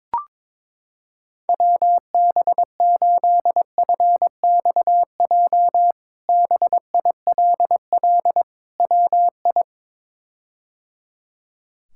Morse code, heard as a keyed tone, about 22 wpm, 700 Hz. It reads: WB8FXJ BILL WI
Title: K1USN SST QSO Practice - 22wpm